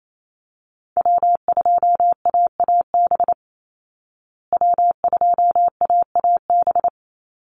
Morse code, heard as a keyed tone, about 28 wpm, 700 Hz. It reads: W2AAB W2AAB
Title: Call Signs Repeated 2x - 28wpm